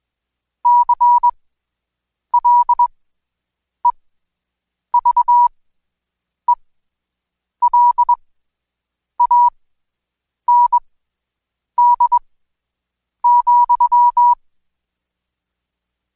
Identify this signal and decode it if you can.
Morse code, heard as a keyed tone, about 20 wpm, spaced as for 7 wpm, 970 Hz. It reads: CLEVELAND,
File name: Code Quick / AudioDownld2 / Seg112-Day17-DayDrill 17